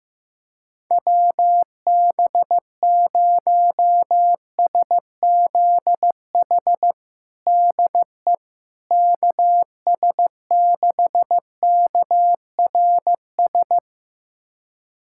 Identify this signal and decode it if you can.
Morse code, heard as a keyed tone, about 15 wpm, 700 Hz. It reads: WB0SZH DE KS6KRS
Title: USA Callsign DE Callsign 15wpm